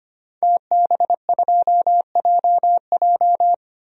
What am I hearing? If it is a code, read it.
Morse code, heard as a keyed tone, about 25 wpm, 700 Hz. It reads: TB2JJ
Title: DXCC-100 Call Signs - 5 Characters - 25wpm